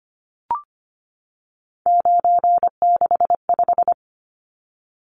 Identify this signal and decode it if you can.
Morse code, heard as a keyed tone, about 25 wpm, 700 Hz. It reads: 965
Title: Numbers - 3-Digit 25wpm